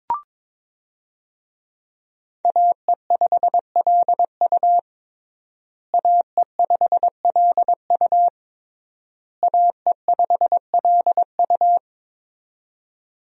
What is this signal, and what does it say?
Morse code, heard as a keyed tone, about 22 wpm, 700 Hz. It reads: AE5LU AE5LU AE5LU
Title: USA Call Signs - 5 Characters - Repeated 3x - 22wpm